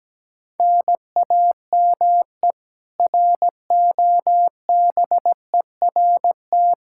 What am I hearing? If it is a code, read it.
Morse code, heard as a keyed tone, about 17 wpm, 700 Hz. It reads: NAME ROBERT
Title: Practice QSO - Exchange 1 - Beginning - 17wpm